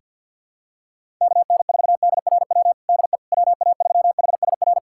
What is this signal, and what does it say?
Morse code, heard as a keyed tone, about 51 wpm, 700 Hz. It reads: KN4DRW DE WA3HSR
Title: USA Callsign DE Callsign 50wpm